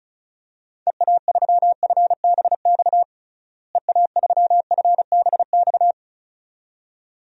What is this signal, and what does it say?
Morse code, heard as a keyed tone, about 36 wpm, 700 Hz. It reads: EA3FBX EA3FBX
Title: DXCC-100 Call Signs - 6 Characters - Repeated 2x - 35wpm